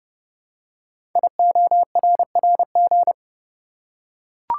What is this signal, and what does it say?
Morse code, heard as a keyed tone, about 30 wpm, 690 Hz. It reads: IORRG
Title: Code Groups - 5 Letters - 30wpm